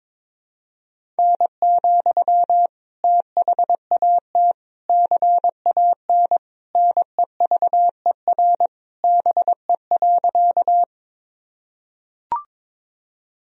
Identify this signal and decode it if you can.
Morse code, heard as a keyed tone, about 22 wpm, 700 Hz. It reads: N, THAT CAN NEVER BE.
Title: Sentences from Top 300 Words and Abbreviations 22wpm